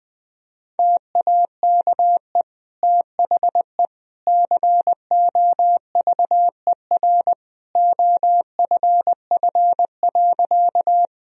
Morse code, heard as a keyed tone, about 20 wpm, 700 Hz. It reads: TAKE THE COVER OFF.
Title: Sentences from Top 1000 Words - Review Part 1 - 20wpm